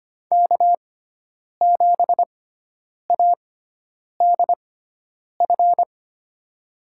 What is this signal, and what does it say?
Morse code, heard as a keyed tone, about 25 wpm, spaced as for 9 wpm, 700 Hz. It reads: K7ADF
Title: USA Call Signs - 6x Character Spacing - 25wpm